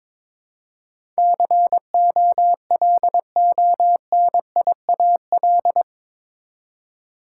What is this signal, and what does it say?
Morse code, heard as a keyed tone, about 22 wpm, 700 Hz. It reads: COLONIAL
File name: practice / 22wpm / Eight Letter Words - 22wpm